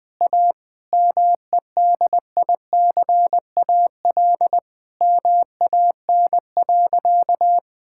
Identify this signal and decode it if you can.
Morse code, heard as a keyed tone, about 20 wpm, 700 Hz. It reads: A MEDICAL MAN.
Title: Sentences from Top 1000 Words - Review Part 5 - 20wpm